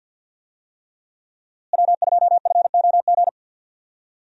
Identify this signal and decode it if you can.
Morse code, heard as a keyed tone, about 46 wpm, 700 Hz. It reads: W2FOG
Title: USA Call Signs - 5 Characters - Repeated 3x - 50wpm